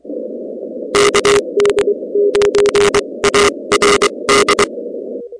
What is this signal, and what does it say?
Morse code, heard as a keyed tone, about 23 wpm, spaced as for 20 wpm, 410 Hz. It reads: KD9ARD
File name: contests / 2019-cw-wpx-kt5j-k5pi- -2330 0-16